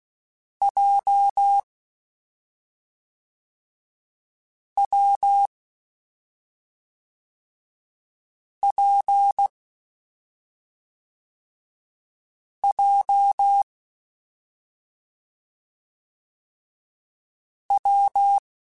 Morse code, heard as a keyed tone, about 16 wpm, 800 Hz. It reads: J W P J W
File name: Difficult Combos / Difficult Combos-mp3s / WPJ WPJ1 @5